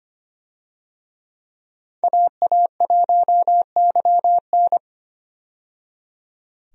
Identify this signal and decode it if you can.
Morse code, heard as a keyed tone, about 25 wpm, 690 Hz. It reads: AA1YN
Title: USA Call Signs - 5 Characters - 25wpm